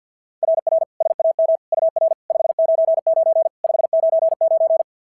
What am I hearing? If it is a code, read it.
Morse code, heard as a keyed tone, about 53 wpm, 690 Hz. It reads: RR SAM UR 599 599
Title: SOTA QSO Practice 50wpm